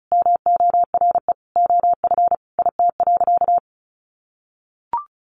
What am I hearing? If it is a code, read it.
Morse code, heard as a keyed tone, about 34 wpm, 700 Hz. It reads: MORE OF IT.
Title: Sentences from Top 1300 Words 35wpm